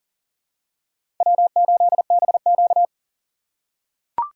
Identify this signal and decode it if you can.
Morse code, heard as a keyed tone, about 40 wpm, 700 Hz. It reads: W8BQ
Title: USA Call Signs - 4x Character Spacing - 40wpm